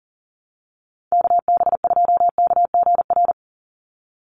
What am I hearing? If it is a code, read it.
Morse code, heard as a keyed tone, about 37 wpm, 700 Hz. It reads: KB2KGR
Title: USA Call Signs 40wpm